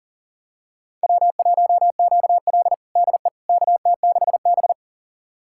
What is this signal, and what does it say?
Morse code, heard as a keyed tone, about 40 wpm, 700 Hz. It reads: W1QL DE KT6B
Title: USA Callsign DE Callsign - 4 Characters - 40wpm